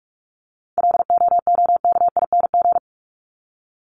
Morse code, heard as a keyed tone, about 44 wpm, 700 Hz. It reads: LOOKING
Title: ING Suffix - Top 100 Words with Base Words 45wpm